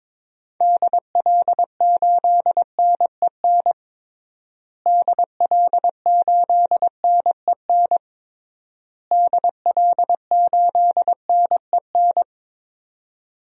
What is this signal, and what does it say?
Morse code, heard as a keyed tone, about 22 wpm, 700 Hz. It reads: DL8NEN DL8NEN DL8NEN
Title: DXCC-100 Call Signs - 6 Characters - Repeated 3x - 22wpm